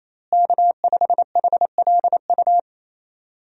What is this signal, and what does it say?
Morse code, heard as a keyed tone, about 28 wpm, 700 Hz. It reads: K5HLU
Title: USA Call Signs - 5 Characters - Repeated 2x - 28wpm